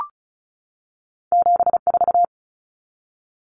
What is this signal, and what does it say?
Morse code, heard as a keyed tone, about 36 wpm, 700 Hz. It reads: 74
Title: Common Cut Numbers - 2-Digit 35wpm